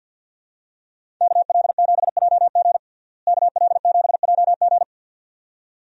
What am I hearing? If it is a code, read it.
Morse code, heard as a keyed tone, about 52 wpm, 700 Hz. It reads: KL7JG KL7JG
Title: DXCC-100 Call Signs - 5 Characters - Repeated 2x - 50wpm